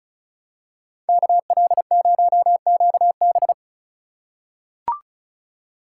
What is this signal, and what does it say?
Morse code, heard as a keyed tone, about 36 wpm, 700 Hz. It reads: KL0QB
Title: USA Call Signs - 5 Characters - 4x Character Spacing - 35wpm